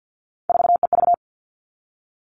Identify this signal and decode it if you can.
Morse code, heard as a keyed tone, about 50 wpm, 700 Hz. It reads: 4EV
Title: All Cut Numbers - 3-Digit 50wpm